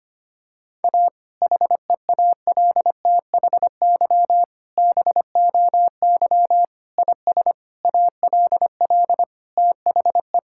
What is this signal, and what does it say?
Morse code, heard as a keyed tone, about 25 wpm, 700 Hz. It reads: A HEALTHY BOY IS ALL THE
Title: Sentences from Top 1000 Words - Review Part 5 - 25wpm